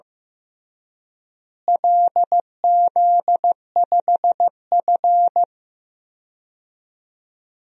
Morse code, heard as a keyed tone, about 15 wpm, 700 Hz. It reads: LZ5F
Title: DXCC-100 Call Signs - 4 Characters - Repeated 3x - 15wpm